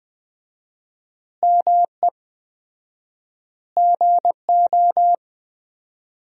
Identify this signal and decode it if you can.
Morse code, heard as a keyed tone, about 20 wpm, 700 Hz. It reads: ME GO
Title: 4x Word-Spacing - Sentences from Top 300 Words - 20wpm